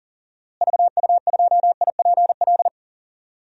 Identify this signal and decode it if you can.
Morse code, heard as a keyed tone, about 40 wpm, 700 Hz. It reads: VU2IPL
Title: DXCC-100 Call Signs - 6 Characters - Repeated 3x - 40wpm